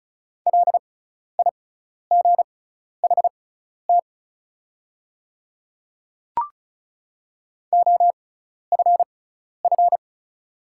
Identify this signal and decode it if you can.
Morse code, heard as a keyed tone, about 36 wpm, spaced as for 12 wpm, 700 Hz. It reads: LIGHT OFF
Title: 6x Character Spacing - Top 300 Words - 35wpm